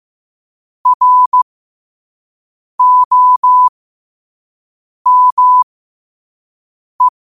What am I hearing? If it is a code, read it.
Morse code, heard as a keyed tone, about 15 wpm, spaced as for 5 wpm, 1000 Hz. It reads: ROME